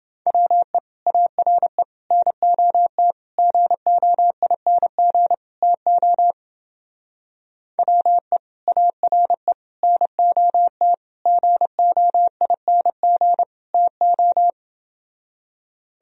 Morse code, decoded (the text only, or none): WE ARE NOT GOING TO WE ARE NOT GOING TO